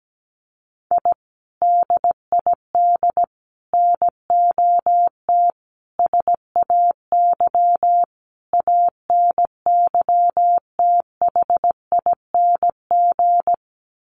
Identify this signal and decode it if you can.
Morse code, heard as a keyed tone, about 17 wpm, 700 Hz. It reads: I DID NOT SAY ANYTHING
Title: Sets of 5 Words Encore2 17wpm